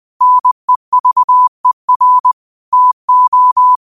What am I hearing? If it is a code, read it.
Morse code, heard as a keyed tone, about 20 wpm, 1000 Hz. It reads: NEVER TO
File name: qotdfiles / 20wpm